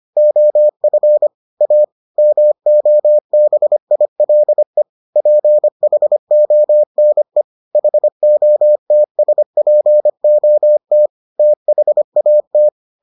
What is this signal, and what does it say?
Morse code, heard as a keyed tone, about 25 wpm, 600 Hz. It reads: OF A MOBILE PHONE HOTSPOT THAT